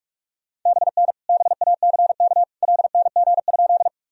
Morse code, heard as a keyed tone, about 46 wpm, 700 Hz. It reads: BN BACK LNG?
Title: Sentences from Top 200 Words and Abbreviations 45wpm